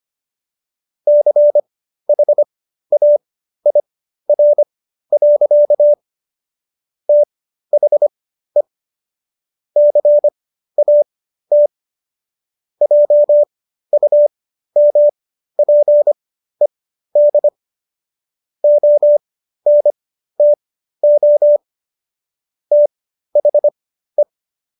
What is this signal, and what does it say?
Morse code, heard as a keyed tone, about 25 wpm, spaced as for 13 wpm, 590 Hz. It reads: CHAIR. THE CAT JUMPED ONTO THE